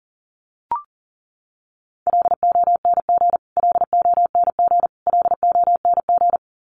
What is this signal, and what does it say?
Morse code, heard as a keyed tone, about 40 wpm, 700 Hz. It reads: LONG LONG LONG
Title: Top 500 Words Repeated 40wpm